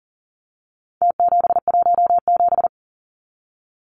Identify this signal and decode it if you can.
Morse code, heard as a keyed tone, about 40 wpm, 700 Hz. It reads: T717
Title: Numbers - Contest Style - 4-Digit 40wpm